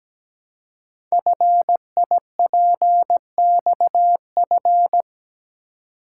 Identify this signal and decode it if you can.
Morse code, heard as a keyed tone, about 17 wpm, 700 Hz. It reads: FIPXF
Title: Code Groups - 5 Letters - 17wpm